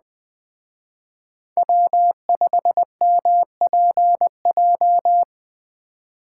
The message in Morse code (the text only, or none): W5MPJ